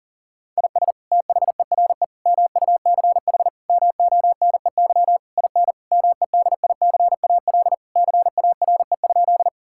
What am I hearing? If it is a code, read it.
Morse code, heard as a keyed tone, about 40 wpm, 700 Hz. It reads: IS THERE MUCH MONEY IN MEDICAL CARE?
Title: Sentences from Top 1000 Words - Review Part 5 - 40wpm